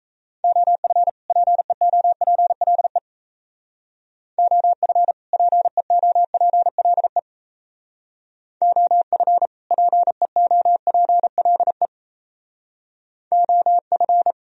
Morse code, heard as a keyed tone, about 34 wpm, 700 Hz. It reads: OF PEOPLE OF PEOPLE OF PEOPLE OF
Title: Speed-Racing - Sets of 2 Words 28wpm